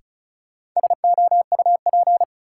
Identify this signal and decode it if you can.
Morse code, heard as a keyed tone, about 36 wpm, 700 Hz. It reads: SOUP
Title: Sentences from Top 1000 Words - Review Part 6 - New Word Drill - 35wpm